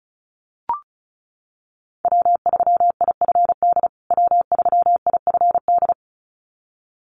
Morse code, heard as a keyed tone, about 34 wpm, 700 Hz. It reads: W3IFD W3IFD
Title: USA Call Signs Repeated 2x 35wpm